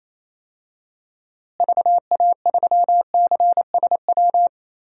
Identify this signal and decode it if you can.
Morse code, heard as a keyed tone, about 28 wpm, 700 Hz. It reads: VA3CSW
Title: DXCC-100 Call Signs - 6 Characters - Repeated 2x - 28wpm